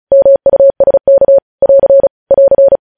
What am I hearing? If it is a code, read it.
Morse code, heard as a keyed tone, about 36 wpm, 550 Hz. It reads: MUSK <AR> <AR>
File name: podcast / News headlines-1x-35x35